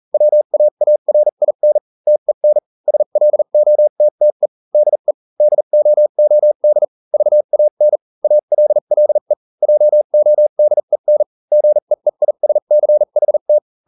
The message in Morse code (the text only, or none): WAARIN TEN SLOTTE DE DOOD VAN ALLE JODEN GEEISCHT